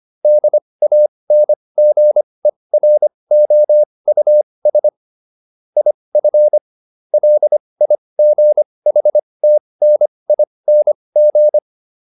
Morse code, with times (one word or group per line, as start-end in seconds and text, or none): 0.2-4.9 DANGEROUS
5.8-6.6 IF
7.1-11.6 LIGHTNING